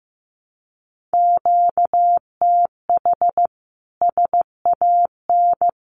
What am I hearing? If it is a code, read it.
Morse code, heard as a keyed tone, about 14 wpm, 700 Hz. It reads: QTH SAN
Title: Common QSO Phrases Encore 15wpm